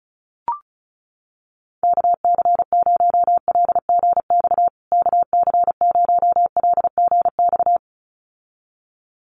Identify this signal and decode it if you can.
Morse code, heard as a keyed tone, about 34 wpm, 700 Hz. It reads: KC0LGX KC0LGX
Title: USA Call Signs Repeated 2x 35wpm